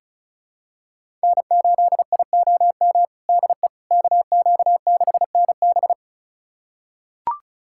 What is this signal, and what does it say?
Morse code, heard as a keyed tone, about 34 wpm, 700 Hz. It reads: N8IOM DE KQ6NB
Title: USA Callsign DE Callsign - 5 Characters - 35wpm